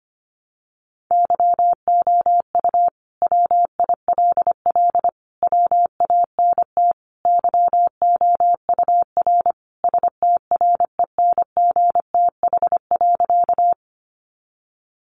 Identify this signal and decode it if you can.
Morse code, heard as a keyed tone, about 25 wpm, 700 Hz. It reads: YOU WILL WANT YOUR STRENGTH.
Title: Sentences from Top 1300 Words 25wpm